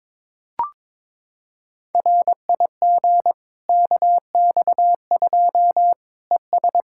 Morse code, heard as a keyed tone, about 22 wpm, 700 Hz. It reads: RIG KX2 ES
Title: K1USN SST QSO Practice 22wpm